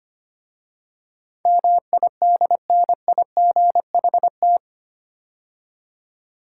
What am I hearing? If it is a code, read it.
Morse code, heard as a keyed tone, about 25 wpm, 700 Hz. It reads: MIDNIGHT
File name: practice / 25wpm / 8 Letter Words - First 3 Letters Given - 25wpm